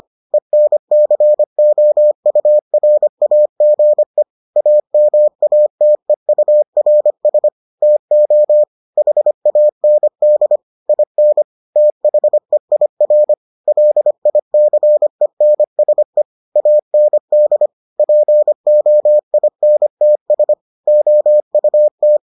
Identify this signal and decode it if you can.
Morse code, heard as a keyed tone, about 25 wpm, 600 Hz. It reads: ENCOURAGE AMATEURS TO HAND IN THEIR LICENSE AND POINTS OUT